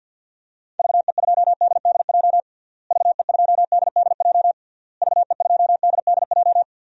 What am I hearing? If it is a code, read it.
Morse code, heard as a keyed tone, about 52 wpm, 700 Hz. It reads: VE2DDJ VE2DDJ VE2DDJ
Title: DXCC-100 Call Signs - 6 Characters - Repeated 3x - 50wpm